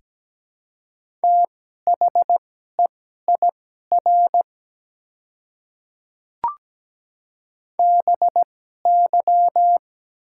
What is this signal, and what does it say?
Morse code, heard as a keyed tone, about 17 wpm, spaced as for 12 wpm, 690 Hz. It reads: THEIR BY
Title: 2x Character Spacing - Top 100 Words - 17wpm